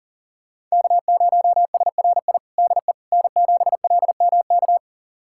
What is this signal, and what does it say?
Morse code, heard as a keyed tone, about 40 wpm, 700 Hz. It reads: K0SRI DE N7LMK
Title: USA Callsign DE Callsign - 5 Characters - 40wpm